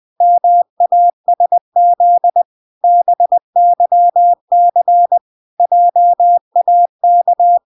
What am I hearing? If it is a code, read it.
Morse code, heard as a keyed tone, about 20 wpm, 700 Hz. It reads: MASZ BYC JAK